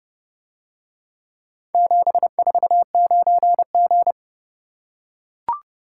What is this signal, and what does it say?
Morse code, heard as a keyed tone, about 28 wpm, 700 Hz. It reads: 749G